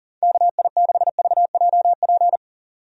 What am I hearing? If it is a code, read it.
Morse code, heard as a keyed tone, about 40 wpm, 700 Hz. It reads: KI6VJP
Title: USA Call Signs - 6 Characters - 40wpm